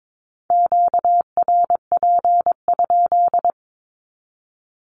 Morse code, heard as a keyed tone, about 22 wpm, 700 Hz. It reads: QRP?
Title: Q-Codes 22wpm